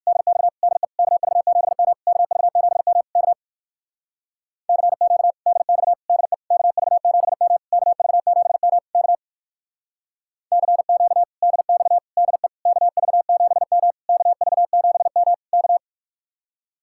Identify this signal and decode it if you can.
Morse code, heard as a keyed tone, about 58 wpm, 700 Hz. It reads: DX DE KV7M KV7M K CQ DX DE KV7M KV7M K CQ DX DE KV7M KV7M K